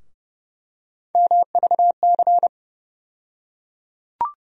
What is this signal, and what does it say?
Morse code, heard as a keyed tone, about 30 wpm, 700 Hz. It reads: MVC
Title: USA Call Sign Postfixes - 3 Letters 30wpm